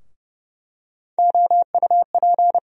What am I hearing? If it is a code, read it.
Morse code, heard as a keyed tone, about 30 wpm, 700 Hz. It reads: OUP
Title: USA Call Sign Postfixes - 3 Letters 30wpm